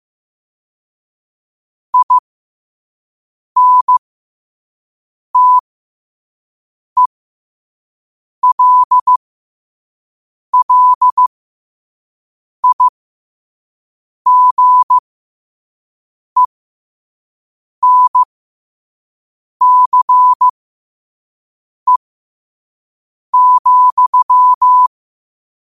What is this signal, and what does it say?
Morse code, heard as a keyed tone, about 15 wpm, spaced as for 5 wpm, 1000 Hz. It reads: INTELLIGENCE,
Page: New quotes every day in morse code at 5 Words per minute.